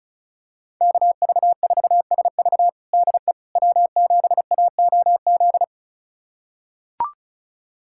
Morse code, heard as a keyed tone, about 36 wpm, 700 Hz. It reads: KV4SV DE W7AOZ
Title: USA Callsign DE Callsign - 5 Characters - 35wpm